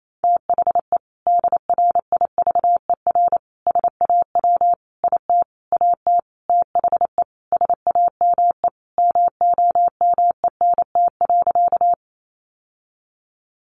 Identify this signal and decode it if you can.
Morse code, heard as a keyed tone, about 28 wpm, 700 Hz. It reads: THE DRIVER SAW IT AT THE SAME MOMENT.
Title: Sentences from Top 1000 Words 28wpm